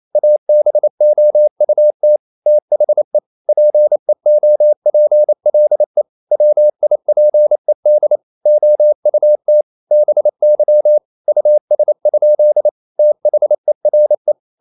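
Morse code, heard as a keyed tone, about 28 wpm, 600 Hz. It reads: ABOUT THE PEOPLE WIPED OUT BY US? THERE
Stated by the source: Morse code practice at 28 wpm, text of Ken Follett's novel Never in English, from CW med Gnister.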